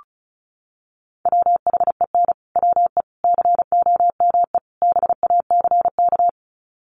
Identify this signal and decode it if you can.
Morse code, heard as a keyed tone, about 34 wpm, 700 Hz. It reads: WHEN WE COME BACK
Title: Sets of 4 Words 35wpm